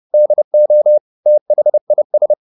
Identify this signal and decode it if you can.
Morse code, heard as a keyed tone, about 30 wpm, 600 Hz. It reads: DO THIS
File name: the old man and the sea / the old man and the sea 1 30wpm